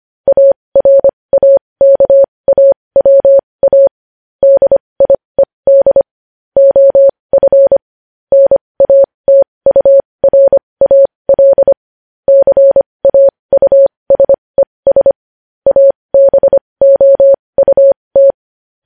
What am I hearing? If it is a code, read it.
Morse code, heard as a keyed tone, about 25 wpm, spaced as for 20 wpm, 550 Hz. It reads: ARAKAWA DIED OF NATURAL CAUSES ABOUT
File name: podcast / News headlines-1x-25x20